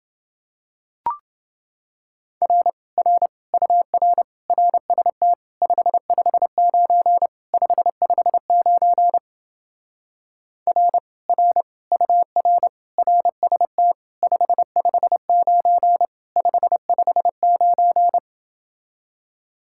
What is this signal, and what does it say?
Morse code, heard as a keyed tone, about 28 wpm, 700 Hz. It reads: R R UR RST 559 559 R R UR RST 559 559